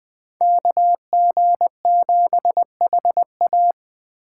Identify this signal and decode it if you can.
Morse code, heard as a keyed tone, about 20 wpm, 700 Hz. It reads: KG7HA
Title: USA Call Signs - 3x Character Spacing - 20wpm